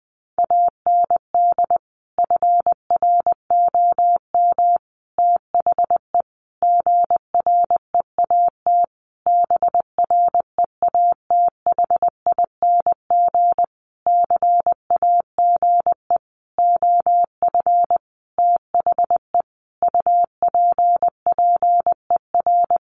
Morse code, heard as a keyed tone, about 20 wpm, 700 Hz. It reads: AND FROM THE GREAT BREATHING CAGE OF THE UPPER
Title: A Sound Of Thunder 20wpm